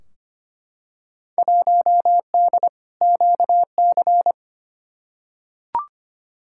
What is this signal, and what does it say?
Morse code, heard as a keyed tone, about 25 wpm, 700 Hz. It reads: 1D QC
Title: Field Day Exchanges - 25wpm